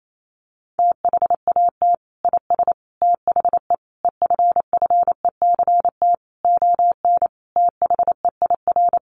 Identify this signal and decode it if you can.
Morse code, heard as a keyed tone, about 28 wpm, 700 Hz. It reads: THAT IS THE EFFECT ON THEIR
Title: Sentences from Top 1300 Words 28wpm